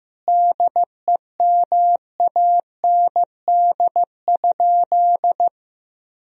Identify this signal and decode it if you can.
Morse code, heard as a keyed tone, about 15 wpm, 700 Hz. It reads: DEMAND?
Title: Sentences from Top 1000 Words - Review Part 4 - 15wpm